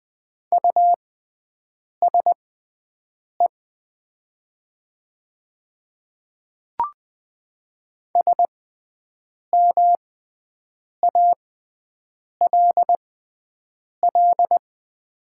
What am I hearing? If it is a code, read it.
Morse code, heard as a keyed tone, about 20 wpm, spaced as for 7 wpm, 700 Hz. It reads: USE SMALL
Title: 6x Character Spacing - Top 200 Words - 20wpm